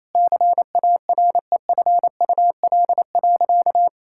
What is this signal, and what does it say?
Morse code, heard as a keyed tone, about 28 wpm, 700 Hz. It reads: CAREFUL.
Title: Sentences from Top 1600 Words 28wpm